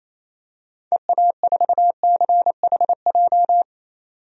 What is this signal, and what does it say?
Morse code, heard as a keyed tone, about 28 wpm, 700 Hz. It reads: EA4CHJ
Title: DXCC-100 Call Signs - 6 Characters - 28wpm